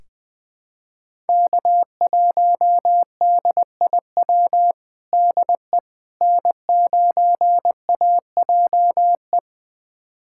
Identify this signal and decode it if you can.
Morse code, heard as a keyed tone, about 21 wpm, 700 Hz. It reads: K1DIW DE N9AJE
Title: USA Callsign DE Callsign - 5 Characters - 20wpm